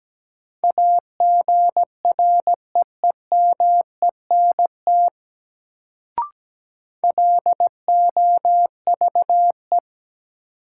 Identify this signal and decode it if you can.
Morse code, heard as a keyed tone, about 17 wpm, 700 Hz. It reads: AGREEMENT LOVE
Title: Sets of 5 Words Encore - New Word Drill - 17wpm